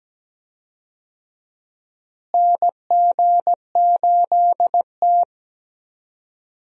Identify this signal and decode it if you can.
Morse code, heard as a keyed tone, about 17 wpm, 700 Hz. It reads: NG8T
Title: USA Call Signs - 4 Characters - Repeated 2x - 17wpm